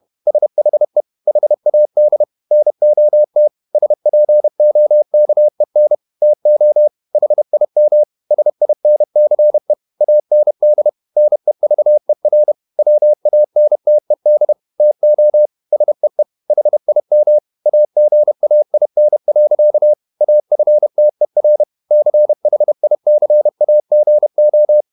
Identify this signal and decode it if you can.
Morse code, heard as a keyed tone, about 31 wpm, 600 Hz. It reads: SHE HAD NOT SPOKEN TO HIM SINCE AND NEVER WANTED TO SEE HIM AGAIN. AFTER CHICAGO
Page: Never 31wpm | CW med Gnister